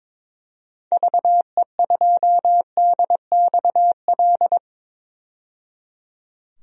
Morse code, heard as a keyed tone, about 22 wpm, 700 Hz. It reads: VE2DXL
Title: DXCC-100 Call Signs - 6 Characters - 22wpm